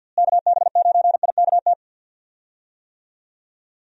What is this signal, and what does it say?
Morse code, heard as a keyed tone, about 52 wpm, 700 Hz. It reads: KB9IKT
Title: USA Call Signs - 6 Characters - Repeated 3x - 50wpm